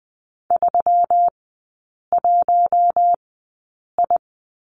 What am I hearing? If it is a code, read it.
Morse code, heard as a keyed tone, about 19 wpm, spaced as for 8 wpm, 700 Hz. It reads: 31I
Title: 121 Lesson 20e - Number 3 - Three Characters 20wpm